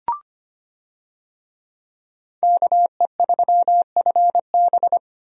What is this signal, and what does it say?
Morse code, heard as a keyed tone, about 25 wpm, 700 Hz. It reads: KE3FB
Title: USA Call Signs - 5 Characters - Repeated 2x - 25wpm